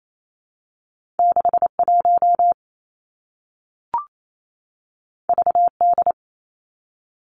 Morse code, heard as a keyed tone, about 28 wpm, 700 Hz. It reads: 6 1 V D
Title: Two Challenging Letter-Number Combinations 28wpm